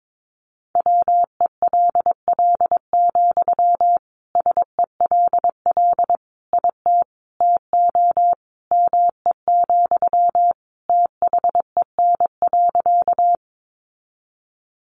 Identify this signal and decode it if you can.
Morse code, heard as a keyed tone, about 22 wpm, 700 Hz. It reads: WELL, SELL IT TO ME, THEN.
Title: Sentences from Top 1100 Words 22wpm